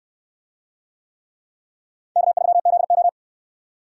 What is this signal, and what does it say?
Morse code, heard as a keyed tone, about 61 wpm, 700 Hz. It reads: K3ZO